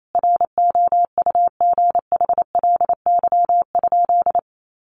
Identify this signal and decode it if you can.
Morse code, heard as a keyed tone, about 28 wpm, 700 Hz. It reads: ROUGHLY?
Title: Sentences from Top 1500 Words 28wpm